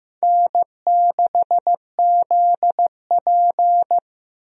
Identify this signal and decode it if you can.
Morse code, heard as a keyed tone, about 15 wpm, 700 Hz. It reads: N6ZP
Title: USA Call Signs - 4 Characters - Repeated 3x - 15wpm